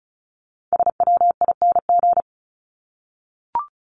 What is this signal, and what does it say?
Morse code, heard as a keyed tone, about 34 wpm, 700 Hz. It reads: SWING
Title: Top 1751-2000 Words 35wpm